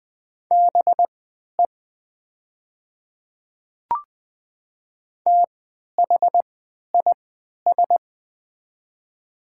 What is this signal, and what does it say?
Morse code, heard as a keyed tone, about 20 wpm, spaced as for 11 wpm, 700 Hz. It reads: BE THIS
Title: 3x Character Spacing - Top 100 Words - 20wpm